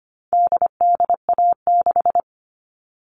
Morse code, heard as a keyed tone, about 25 wpm, 700 Hz. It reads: DDA6
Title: All Cut Numbers - 4-Digit 25wpm